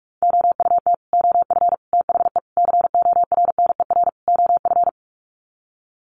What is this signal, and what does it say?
Morse code, heard as a keyed tone, about 45 wpm, 700 Hz. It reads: OUT OF THE CORNER OF
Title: Sets of 5 Words 45wpm